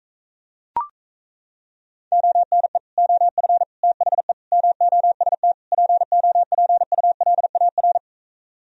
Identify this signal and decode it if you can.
Morse code, heard as a keyed tone, about 37 wpm, 700 Hz. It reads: ONE OF THE MOST POPULAR